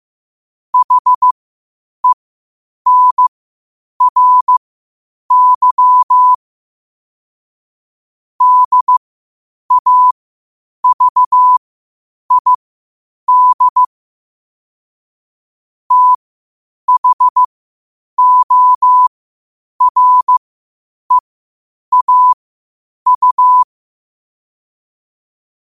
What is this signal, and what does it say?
Morse code, heard as a keyed tone, about 15 wpm, spaced as for 8 wpm, 1000 Hz. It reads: HENRY DAVID THOREAU
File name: 8wpm